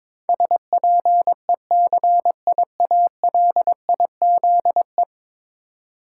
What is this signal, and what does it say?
Morse code, heard as a keyed tone, about 22 wpm, 700 Hz. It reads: SPECIALIZE
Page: morse-code-ninja-practice-22wpm